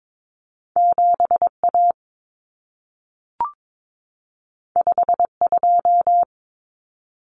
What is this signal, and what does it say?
Morse code, heard as a keyed tone, about 22 wpm, 700 Hz. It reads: 7A 52
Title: Common Cut Numbers - 2-Digit 22wpm